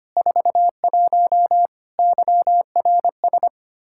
Morse code, heard as a keyed tone, about 25 wpm, 700 Hz. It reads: 41 YRS
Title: Practice QSO - Exchange 3 - More Info - 25wpm